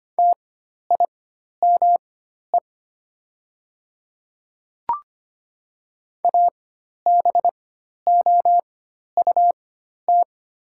Morse code, heard as a keyed tone, about 25 wpm, spaced as for 12 wpm, 700 Hz. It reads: TIME ABOUT
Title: 4x Character Spacing - Top 100 Words - 25wpm